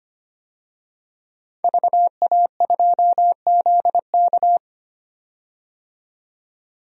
Morse code, heard as a keyed tone, about 25 wpm, 700 Hz. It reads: VA2ZK
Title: DXCC-100 Call Signs - 5 Characters - 25wpm